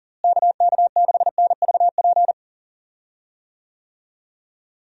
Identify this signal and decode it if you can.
Morse code, heard as a keyed tone, about 40 wpm, 700 Hz. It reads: KK6NVP
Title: USA Call Signs - 6 Characters - Repeated 3x - 40wpm